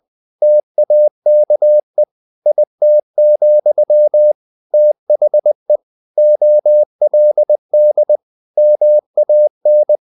Morse code, heard as a keyed tone, about 20 wpm, 600 Hz. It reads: TAKE IT, THE OLD MAN
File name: the old man and the sea / the old man and the sea part 1 20wpm